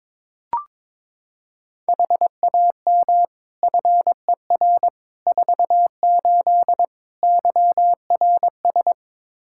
Morse code, heard as a keyed tone, about 22 wpm, 700 Hz. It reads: HAM FER 48 YRS
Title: Practice QSO - Exchange 3 - More Info - 22wpm